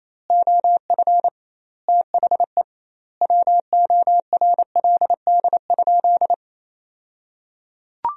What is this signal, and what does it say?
Morse code, heard as a keyed tone, about 28 wpm, 700 Hz. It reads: OF THE WORLD?
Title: 2x Word-Spacing - Sentences from Top 200 Words - 28wpm